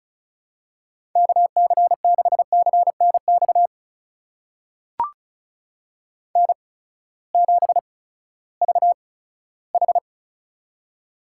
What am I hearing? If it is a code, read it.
Morse code, heard as a keyed tone, about 36 wpm, spaced as for 18 wpm, 700 Hz. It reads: KC6CNX N7VH
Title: USA Call Signs - 8x Character Spacing - 35wpm